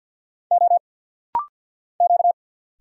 Morse code, heard as a keyed tone, about 50 wpm, 700 Hz. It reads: O Q
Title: Single Letter-Number Alt - Rapid-Fire 50wpm